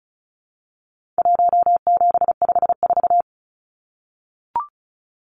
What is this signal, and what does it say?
Morse code, heard as a keyed tone, about 34 wpm, 700 Hz. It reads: 1754
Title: Numbers - Contest Style - 4-Digit 35wpm